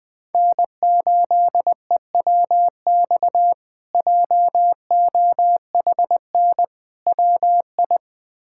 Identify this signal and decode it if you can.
Morse code, heard as a keyed tone, about 20 wpm, 700 Hz. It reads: N8EWX JOHN WI
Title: K1USN SST QSO Practice - 20wpm